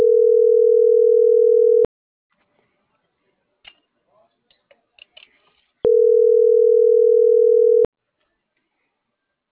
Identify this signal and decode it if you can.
Ringback tone (North America), 440 + 480 Hz.